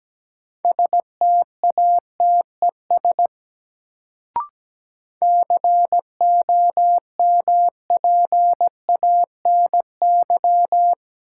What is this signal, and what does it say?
Morse code, heard as a keyed tone, about 17 wpm, 700 Hz. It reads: STATES COMPANY
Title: Sets of 2 Words - New Word Drill - 17wpm